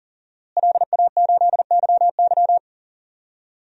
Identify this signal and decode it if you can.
Morse code, heard as a keyed tone, about 40 wpm, 700 Hz. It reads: LA8YY
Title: DXCC-100 Call Signs - 5 Characters - Repeated 3x - 40wpm